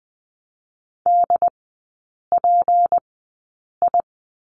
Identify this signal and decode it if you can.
Morse code, heard as a keyed tone, about 20 wpm, spaced as for 8 wpm, 700 Hz. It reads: DPI
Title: 166 Lesson 27e - Number 7 - Three Characters 20wpm